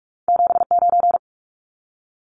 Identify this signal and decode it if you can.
Morse code, heard as a keyed tone, about 46 wpm, 700 Hz. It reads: 79
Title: Numbers - 2-Digit 45wpm